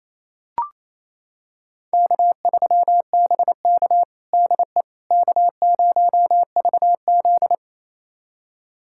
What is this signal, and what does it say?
Morse code, heard as a keyed tone, about 27 wpm, 700 Hz. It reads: K3BK DE K0VZ
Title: USA Callsign DE Callsign - 4 Characters - 28wpm